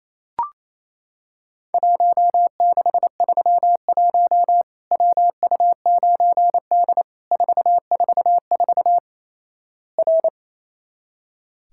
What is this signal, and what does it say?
Morse code, heard as a keyed tone, about 28 wpm, 700 Hz. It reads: 1631 WU9D 444 R
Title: Worked All Europe DX Contest - QTC Exchange - 28wpm